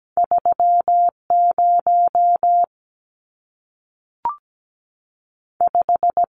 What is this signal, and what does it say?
Morse code, heard as a keyed tone, about 17 wpm, 700 Hz. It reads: 3 0 5
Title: Numbers - 2-Digit 17wpm